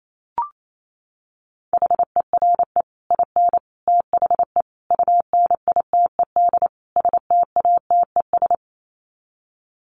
Morse code, heard as a keyed tone, about 28 wpm, 700 Hz. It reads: HERE IN THE UNITED STATES
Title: Sets of 5 Words 28wpm